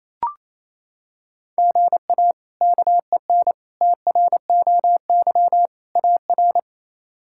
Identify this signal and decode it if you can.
Morse code, heard as a keyed tone, about 28 wpm, 700 Hz. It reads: GA KEN TROY AR
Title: K1USN SST QSO Practice - 28wpm